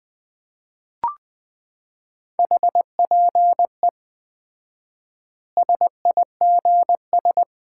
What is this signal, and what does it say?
Morse code, heard as a keyed tone, about 20 wpm, 700 Hz. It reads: HPE SIGS
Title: Common QSO Phrases - 4x Word-Spacing - 20wpm